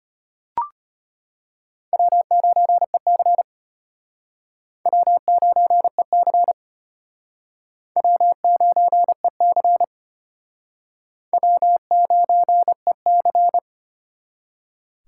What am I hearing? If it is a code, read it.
Morse code, heard as a keyed tone, about 33 wpm, 700 Hz. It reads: W9EC W9EC W9EC W9EC